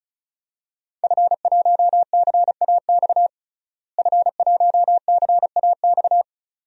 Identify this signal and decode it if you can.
Morse code, heard as a keyed tone, about 36 wpm, 700 Hz. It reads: F1CAX F1CAX
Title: DXCC-100 Call Signs - 5 Characters - Repeated 2x - 35wpm